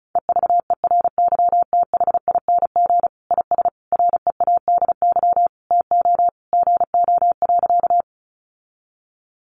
Morse code, heard as a keyed tone, about 34 wpm, 700 Hz. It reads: EVERYTHING IS READY TO GO.
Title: Sentences from Top 700 Words 35wpm